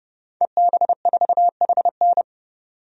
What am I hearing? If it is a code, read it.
Morse code, heard as a keyed tone, about 31 wpm, 700 Hz. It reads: EB4HN
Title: DXCC-100 Call Signs - 5 Characters - 30wpm